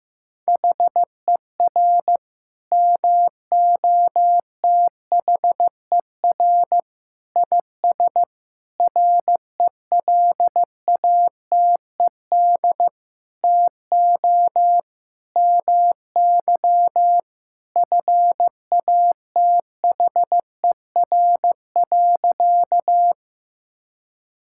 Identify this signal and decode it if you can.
Morse code, heard as a keyed tone, about 15 wpm, 700 Hz. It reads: HER MOTHER IS RELATED TO MY FATHER.
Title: Sentences from Top 1000 Words - Review Part 5 - 15wpm